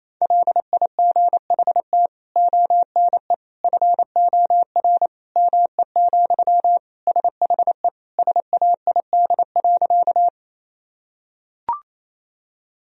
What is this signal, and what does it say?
Morse code, heard as a keyed tone, about 27 wpm, 700 Hz. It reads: LIGHT ONE FOR ME, SHE SAID.
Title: Sentences from Top 1000 Words - Review Part 1 - 28wpm